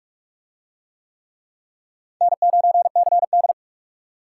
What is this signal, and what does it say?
Morse code, heard as a keyed tone, about 44 wpm, 700 Hz. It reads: N9CD